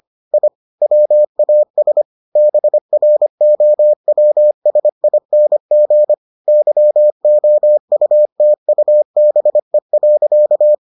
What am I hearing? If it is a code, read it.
Morse code, heard as a keyed tone, about 25 wpm, 600 Hz. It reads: I WAS BROWSING YOUTUBE.